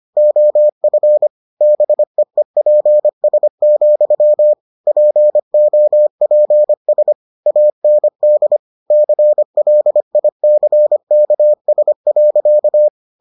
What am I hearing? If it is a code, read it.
Morse code, heard as a keyed tone, about 25 wpm, 600 Hz. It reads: OF BEEPS, POPS AND CLICKS.